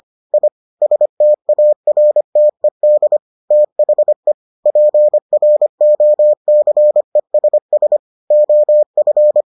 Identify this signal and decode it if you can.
Morse code, heard as a keyed tone, about 25 wpm, 600 Hz. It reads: I STARTED THE PROCESS OF